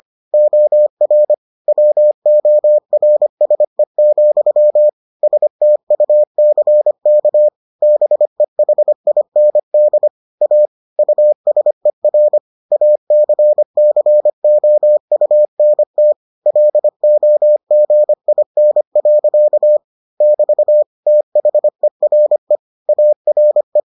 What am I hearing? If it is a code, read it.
Morse code, heard as a keyed tone, about 25 wpm, 600 Hz. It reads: OR WORSE, STUCK BEHIND A USER ACCOUNT LOGIN. <BT> THERE ARE